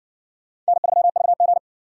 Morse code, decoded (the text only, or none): N3VG